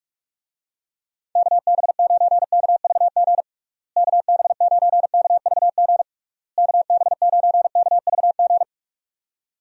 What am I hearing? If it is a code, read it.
Morse code, heard as a keyed tone, about 41 wpm, 700 Hz. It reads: KB9KVG KB9KVG KB9KVG
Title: USA Call Signs - 6 Characters - Repeated 3x - 45wpm